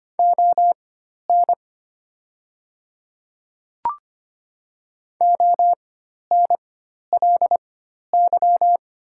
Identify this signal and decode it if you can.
Morse code, heard as a keyed tone, about 25 wpm, spaced as for 12 wpm, 700 Hz. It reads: ON ONLY
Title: 4x Character Spacing - Top 300 Words - 25wpm